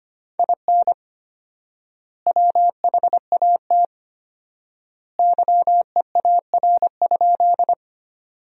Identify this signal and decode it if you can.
Morse code, heard as a keyed tone, about 25 wpm, 700 Hz. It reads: IN WHAT YEAR?
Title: 4x Word-Spacing - Sentences from Top 200 Words - 25wpm